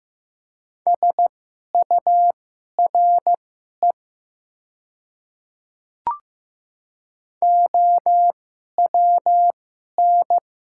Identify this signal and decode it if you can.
Morse code, heard as a keyed tone, about 15 wpm, spaced as for 11 wpm, 700 Hz. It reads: SURE OWN
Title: 2x Character Spacing - Top 300 Words - 15wpm